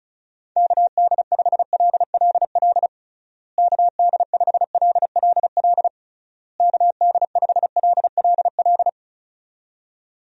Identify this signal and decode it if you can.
Morse code, heard as a keyed tone, about 34 wpm, 700 Hz. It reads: KD5LLL KD5LLL KD5LLL
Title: USA Call Signs - 6 Characters - Repeated 3x - 35wpm